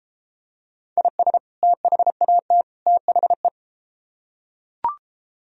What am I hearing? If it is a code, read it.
Morse code, heard as a keyed tone, about 32 wpm, 700 Hz. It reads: IS THAT THE